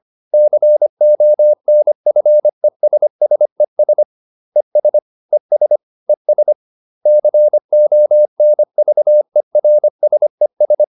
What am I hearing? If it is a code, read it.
Morse code, heard as a keyed tone, about 25 wpm, 600 Hz. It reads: CONFESSES ES ES ES CONVERSES